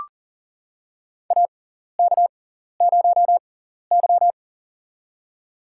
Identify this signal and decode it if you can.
Morse code, heard as a keyed tone, about 40 wpm, spaced as for 14 wpm, 700 Hz. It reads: AK0Y
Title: USA Call Signs - 6x Character Spacing - 40wpm